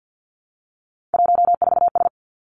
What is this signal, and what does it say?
Morse code, heard as a keyed tone, about 50 wpm, 700 Hz. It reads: 14S
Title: Three Challenging Letter-Number Combinations 50wpm